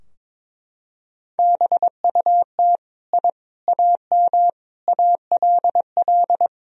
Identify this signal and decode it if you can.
Morse code, heard as a keyed tone, about 21 wpm, 700 Hz. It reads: BUT I AM ALL
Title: Sentences from Top 1000 Words - Review Part 2 - 22wpm